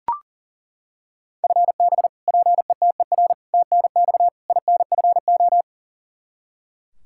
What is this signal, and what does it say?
Morse code, heard as a keyed tone, about 37 wpm, 700 Hz. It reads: FB PETER TNX INFO
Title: Practice QSO - Exchange 3 - More Info - 40wpm